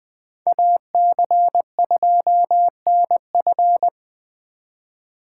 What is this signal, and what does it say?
Morse code, heard as a keyed tone, about 25 wpm, 700 Hz. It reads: AC2NF
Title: Speed Racing - USA 2x2 Call Signs 20wpm